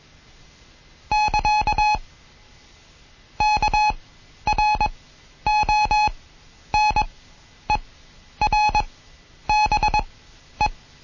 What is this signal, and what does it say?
Morse code, heard as a keyed tone, about 21 wpm, spaced as for 10 wpm, 850 Hz. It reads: <KA> KRONERBE
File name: Morsen / Kurs A 20